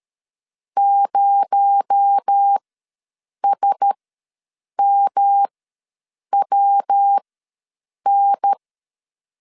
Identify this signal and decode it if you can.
Morse code, heard as a keyed tone, about 13 wpm, spaced as for 7 wpm, 800 Hz. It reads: 0SMWN
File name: LEZIONI / 7 GF9Z / 7 GF ws 12wpm 3140